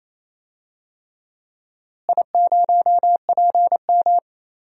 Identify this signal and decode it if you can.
Morse code, heard as a keyed tone, about 28 wpm, 700 Hz. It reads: I0PM
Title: DXCC-100 Call Signs - 4 Characters - Repeated 3x - 28wpm